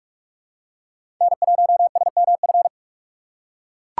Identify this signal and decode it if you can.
Morse code, heard as a keyed tone, about 44 wpm, 700 Hz. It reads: N1SMF
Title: USA Call Signs - 6x Character Spacing - 45wpm